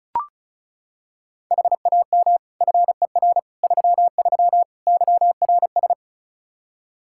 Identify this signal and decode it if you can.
Morse code, heard as a keyed tone, about 36 wpm, 700 Hz. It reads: HAM FER 33 YRS
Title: Practice QSO - Exchange 3 - More Info - 35wpm